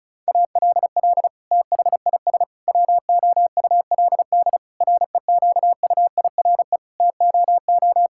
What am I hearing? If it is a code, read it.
Morse code, heard as a keyed tone, about 36 wpm, 700 Hz. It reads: ALL THIS WOULD REQUIRE TOO
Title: Sentences from Top 1000 Words - Review Part 6 - 35wpm